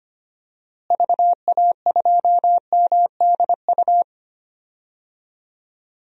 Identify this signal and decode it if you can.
Morse code, heard as a keyed tone, about 25 wpm, 700 Hz. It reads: VA2MDU
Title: DXCC-100 Call Signs - 6 Characters - Repeated 3x - 25wpm